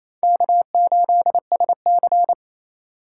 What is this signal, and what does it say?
Morse code, heard as a keyed tone, about 28 wpm, 700 Hz. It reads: K8SC
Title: USA Call Signs - 4 Characters - 3x Character Spacing - 28wpm